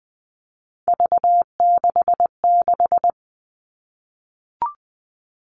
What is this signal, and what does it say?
Morse code, heard as a keyed tone, about 20 wpm, 700 Hz. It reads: V66
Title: All Cut Numbers - 3-Digit 20wpm